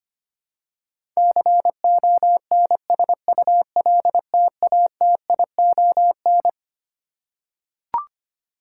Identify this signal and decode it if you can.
Morse code, heard as a keyed tone, about 25 wpm, 700 Hz. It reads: CONSULTATION
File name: Twelve Letter Words - 25wpm